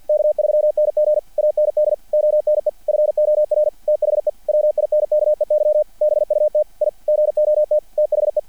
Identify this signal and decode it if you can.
Morse code, heard as a keyed tone, about 50 wpm, 600 Hz. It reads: K3NG AND ONE FOR THE WINKEY BUT I GOT THE
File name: archived / CRSnet